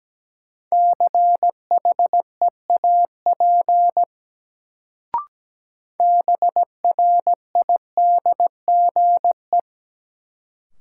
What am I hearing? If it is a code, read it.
Morse code, heard as a keyed tone, about 17 wpm, 700 Hz. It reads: CHEAP BRIDGE
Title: Sentences from Top 1200 Words - New Word Drill - 17wpm